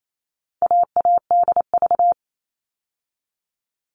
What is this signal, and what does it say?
Morse code, heard as a keyed tone, about 28 wpm, 700 Hz. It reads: AADV
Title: All Cut Numbers - 4-Digit 28wpm